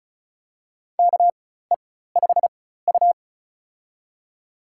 Morse code, heard as a keyed tone, about 36 wpm, spaced as for 16 wpm, 700 Hz. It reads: KE5U
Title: USA Call Signs - 4 Characters - 4x Character Spacing - 35wpm